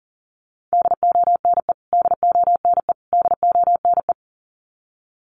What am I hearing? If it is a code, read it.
Morse code, heard as a keyed tone, about 37 wpm, 700 Hz. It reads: DONE DONE DONE
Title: Top 500 Words Repeated 40wpm